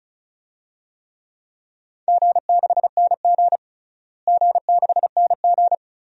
Morse code, heard as a keyed tone, about 36 wpm, 700 Hz. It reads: G6NG G6NG
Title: DXCC-100 Call Signs - 4 Characters - Repeated 2x - 35wpm